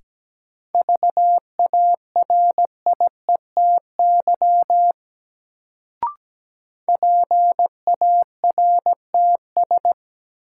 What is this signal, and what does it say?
Morse code, heard as a keyed tone, about 17 wpm, 700 Hz. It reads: VARIETY PARTS
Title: Sets of 3 Words - New Word Drill - 17wpm